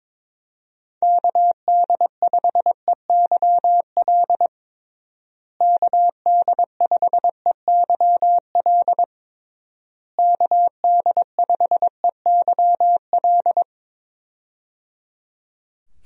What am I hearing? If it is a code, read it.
Morse code, heard as a keyed tone, about 22 wpm, 700 Hz. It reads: KD5EYL KD5EYL KD5EYL
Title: USA Call Signs - 6 Characters - Repeated 3x - 22wpm